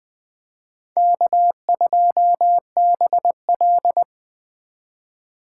Morse code, heard as a keyed tone, about 20 wpm, 700 Hz. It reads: K2BL
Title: Speed Racing - USA Call Signs 17wpm